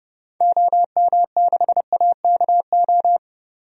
Morse code, heard as a keyed tone, about 30 wpm, 700 Hz. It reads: OM6AKO
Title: DXCC-100 Call Signs - 6 Characters - 30wpm